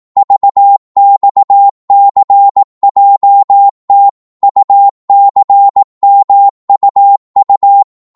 Morse code, heard as a keyed tone, about 18 wpm, 800 Hz. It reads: VXCJT UCMUU
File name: Morse / ltr-18wpm